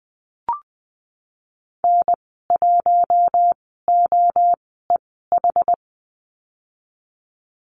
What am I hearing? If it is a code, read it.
Morse code, heard as a keyed tone, about 19 wpm, spaced as for 14 wpm, 700 Hz. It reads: N1OEH
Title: 062 Lesson 11g - Letter h - Call Signs 20wpm